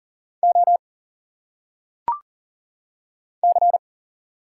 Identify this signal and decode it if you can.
Morse code, heard as a keyed tone, about 40 wpm, 700 Hz. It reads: O C
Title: Single Letters - Phonetic Pronunciation 40wpm